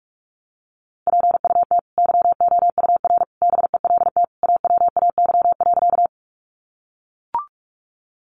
Morse code, heard as a keyed tone, about 44 wpm, 700 Hz. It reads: PUT YOUR BELT AWAY.
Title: Sentences from Top 1400 Words 45wpm